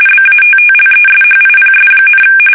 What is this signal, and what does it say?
RTTY, about 45.45 baud. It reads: BATTLESHIPS,
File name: fsk850 / battleshipnj